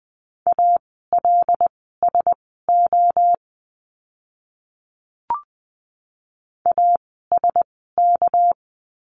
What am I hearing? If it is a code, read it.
Morse code, heard as a keyed tone, about 20 wpm, spaced as for 15 wpm, 700 Hz. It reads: ALSO ASK
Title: 230 Lesson 37f - Letter x - Words 20wpm